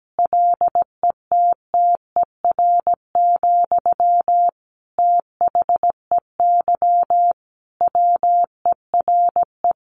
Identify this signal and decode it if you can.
Morse code, heard as a keyed tone, about 17 wpm, 700 Hz. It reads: LETTER, THEY WERE
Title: Sentences from Top 1500 Words 17wpm